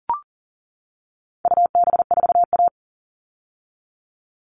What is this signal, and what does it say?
Morse code, heard as a keyed tone, about 40 wpm, 700 Hz. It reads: UB4A
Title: All Cut Numbers - 4-Digit 40wpm